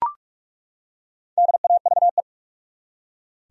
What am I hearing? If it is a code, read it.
Morse code, heard as a keyed tone, about 48 wpm, 700 Hz. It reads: DAVE
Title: USA Common Names - 45wpm